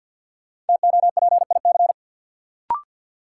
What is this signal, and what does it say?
Morse code, heard as a keyed tone, about 53 wpm, 700 Hz. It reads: TOPIC
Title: Sentences from Top 1200 Words - New Word Drill - 50wpm